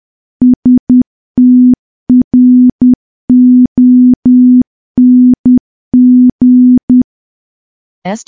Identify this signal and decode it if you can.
Morse code, heard as a keyed tone, about 10 wpm, 250 Hz. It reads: STRONG
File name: morse / morse-training-10WPM